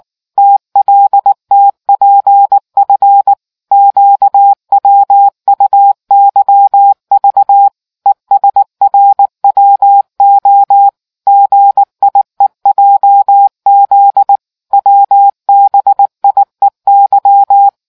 Morse code, heard as a keyed tone, about 19 wpm, 800 Hz. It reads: TLTPF QWUYV ESRWO GIEJZ WBIEY